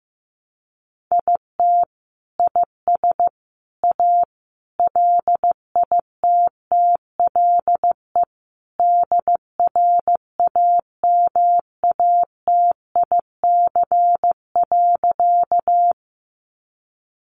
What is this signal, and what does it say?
Morse code, heard as a keyed tone, about 15 wpm, 700 Hz. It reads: IT IS A LITTLE DRAMATIC.
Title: Sentences from Top 1300 Words 15wpm